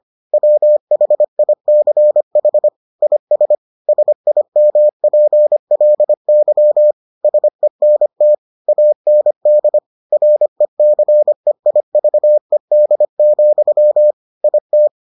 Morse code, heard as a keyed tone, about 25 wpm, 600 Hz. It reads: WHICH IS SIMPLY SENT AND RECEIVED, IT